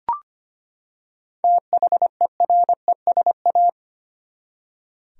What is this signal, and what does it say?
Morse code, heard as a keyed tone, about 25 wpm, 700 Hz. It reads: THERESA
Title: USA Common Names - 25wpm